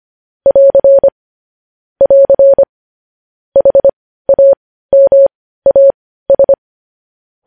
Morse code, heard as a keyed tone, about 25 wpm, spaced as for 15 wpm, 550 Hz. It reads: <AR> <AR> HAMAS
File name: podcast / News headlines-1x-25x15